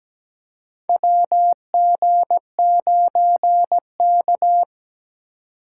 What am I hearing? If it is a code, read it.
Morse code, heard as a keyed tone, about 17 wpm, 700 Hz. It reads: WG9K
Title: USA Call Signs - 3x Character Spacing - 17wpm